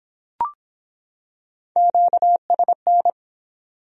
Morse code, heard as a keyed tone, about 23 wpm, 700 Hz. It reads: QSN